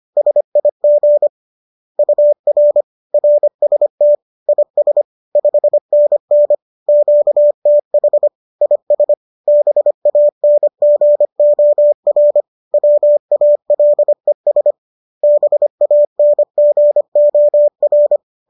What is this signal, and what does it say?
Morse code, heard as a keyed tone, about 25 wpm, 590 Hz. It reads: SIG UR RST IS 5NN QTH IS BANGOR WALES BANGOR